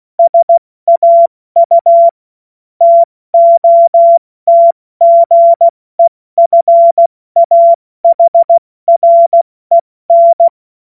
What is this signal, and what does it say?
Morse code, heard as a keyed tone, about 16 wpm, 680 Hz. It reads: SAU TOTGEFAHREN
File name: Morsekurs / Witze / Witz 03 Juni Mai 2025 70 F80